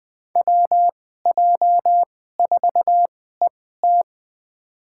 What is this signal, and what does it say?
Morse code, heard as a keyed tone, about 20 wpm, spaced as for 14 wpm, 700 Hz. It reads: WJ4ET
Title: USA Call Signs - 2x Character Spacing - 20wpm